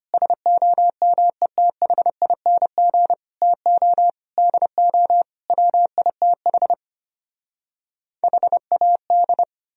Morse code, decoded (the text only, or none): SOMETHING TO DO WITH HAD